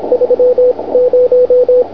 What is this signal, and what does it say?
Morse code, heard as a keyed tone, about 26 wpm, spaced as for 21 wpm, 500 Hz, with radio noise. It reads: 30